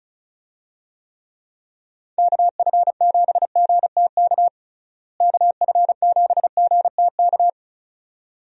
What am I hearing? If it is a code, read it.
Morse code, heard as a keyed tone, about 36 wpm, 700 Hz. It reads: KF7GTK KF7GTK
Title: USA Call Signs - 6 Characters - Repeated 2x - 35wpm